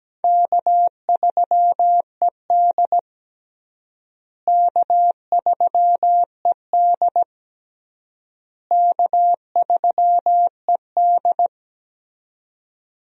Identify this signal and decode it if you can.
Morse code, heard as a keyed tone, about 17 wpm, 700 Hz. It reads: K3ED K3ED K3ED
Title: USA Call Signs - 4 Characters - Repeated 3x - 17wpm